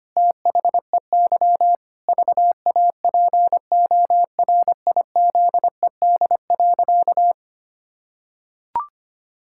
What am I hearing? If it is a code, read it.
Morse code, heard as a keyed tone, about 25 wpm, 700 Hz. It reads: THEY VAPORIZED.
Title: Pangram Challenge v2 - 25wpm